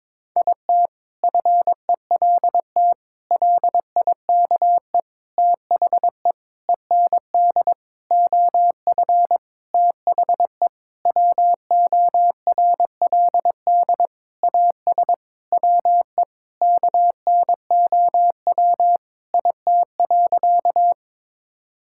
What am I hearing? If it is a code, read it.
Morse code, heard as a keyed tone, about 21 wpm, 700 Hz. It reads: IT FELT LIKE THE END OF THE WORLD AS WE KNOW IT.
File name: practice / 22wpm / Sentences with 8-Word Phrases - 22wpm